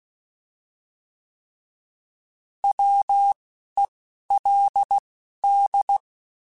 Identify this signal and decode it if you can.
Morse code, heard as a keyed tone, about 16 wpm, 800 Hz. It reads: W E L D
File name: Alphabetical / Alphabetical mp3s / Ws @10